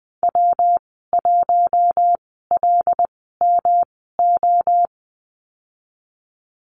Morse code, heard as a keyed tone, about 19 wpm, spaced as for 14 wpm, 700 Hz. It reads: W1LMO
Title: 154 Lesson 25g - Letter p - Call Signs 20wpm